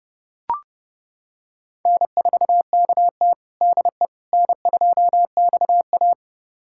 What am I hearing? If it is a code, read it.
Morse code, heard as a keyed tone, about 30 wpm, 700 Hz. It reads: N4KT DE N2XA
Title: USA Callsign DE Callsign - 4 Characters - 30wpm